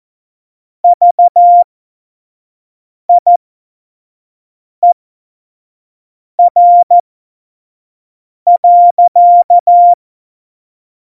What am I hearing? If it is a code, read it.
Morse code, heard as a keyed tone, about 14 wpm, spaced as for 5 wpm, 700 Hz. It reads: VIER.
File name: Morsekurs / Witze / Witz 13. Oktober 2020, 25 F70